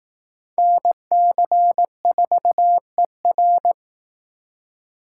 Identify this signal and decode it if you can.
Morse code, heard as a keyed tone, about 18 wpm, 700 Hz. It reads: NC4ER
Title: Speed Racing - USA Call Signs 15wpm